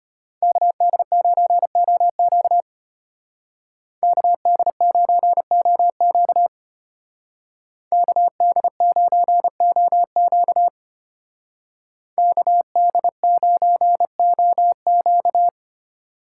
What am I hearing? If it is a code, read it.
Morse code, heard as a keyed tone, about 30 wpm, 700 Hz. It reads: KD9OQ KD9OQ KD9OQ KD9OQ
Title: Speed Racing - USA 2x2 Call Signs 25wpm